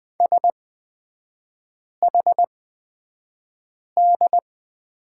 Morse code, heard as a keyed tone, about 20 wpm, spaced as for 5 wpm, 700 Hz. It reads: SHD
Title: 250 Lesson 40e - BK - Three Characters 20wpm